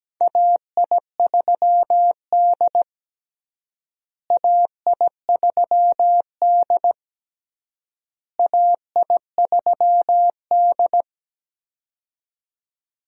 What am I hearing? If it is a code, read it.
Morse code, heard as a keyed tone, about 17 wpm, 700 Hz. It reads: AI3D AI3D AI3D
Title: USA Call Signs - 4 Characters - Repeated 3x - 17wpm